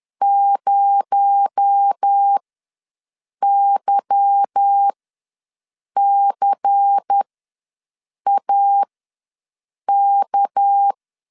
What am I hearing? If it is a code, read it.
Morse code, heard as a keyed tone, about 10 wpm, spaced as for 6 wpm, 800 Hz. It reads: 0YCAK